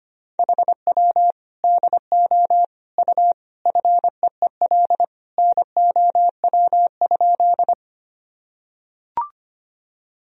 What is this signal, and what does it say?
Morse code, heard as a keyed tone, about 25 wpm, 700 Hz. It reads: HW DO U FEEL NOW?
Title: Sentences from Top 400 Words and Abbreviations 25wpm